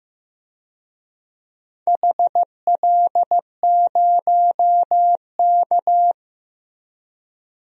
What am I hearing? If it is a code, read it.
Morse code, heard as a keyed tone, about 15 wpm, 700 Hz. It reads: HL0K
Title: DXCC-100 Call Signs - 4 Characters - Repeated 3x - 15wpm